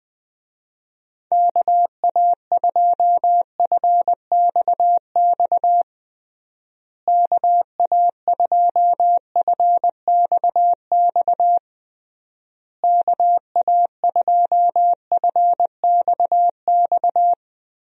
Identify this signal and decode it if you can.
Morse code, heard as a keyed tone, about 20 wpm, 700 Hz. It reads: KA2FXX KA2FXX KA2FXX
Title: USA Call Signs - 6 Characters - Repeated 3x - 20wpm